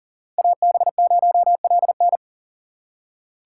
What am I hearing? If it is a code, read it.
Morse code, heard as a keyed tone, about 40 wpm, 700 Hz. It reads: AB0LN
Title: USA Call Signs - 5 Characters - 6x Character Spacing - 40wpm